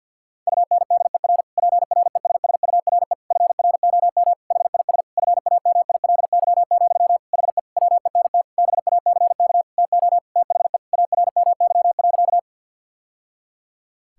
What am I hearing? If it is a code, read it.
Morse code, heard as a keyed tone, about 50 wpm, 700 Hz. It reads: UNDER PRESSURE FROM HIS FAMILY, HE WENT BACK TO THE ARMY.
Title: Sentences from Top 1000 Words - Review Part 6 - 50wpm